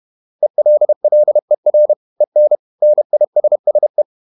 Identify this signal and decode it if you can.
Morse code, heard as a keyed tone, about 31 wpm, 600 Hz. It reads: ELLER EN NISSE
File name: Ferie QRQ 31wpm